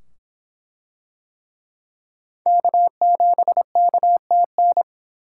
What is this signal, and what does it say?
Morse code, heard as a keyed tone, about 24 wpm, 700 Hz. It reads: K7KTN